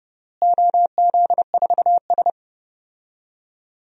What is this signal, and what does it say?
Morse code, heard as a keyed tone, about 30 wpm, 700 Hz. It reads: OZ4S
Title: DXCC-100 Call Signs - 4 Characters - Repeated 3x - 30wpm